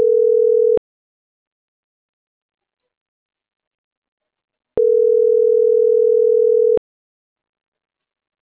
Ringback tone (North America), 440 + 480 Hz.